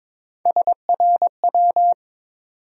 Morse code, heard as a keyed tone, about 22 wpm, 700 Hz. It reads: SRW
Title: USA Call Sign Postfixes - 3 Letters 22wpm